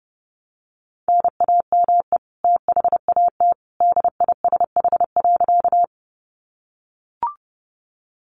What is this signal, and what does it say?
Morse code, heard as a keyed tone, about 30 wpm, 700 Hz. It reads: NAME THAT DISH.
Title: Sentences from Top 1300 Words 30wpm